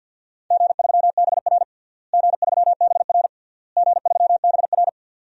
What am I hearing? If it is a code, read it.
Morse code, heard as a keyed tone, about 51 wpm, 700 Hz. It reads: G3BR G3BR G3BR
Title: DXCC-100 Call Signs - 4 Characters - Repeated 3x - 50wpm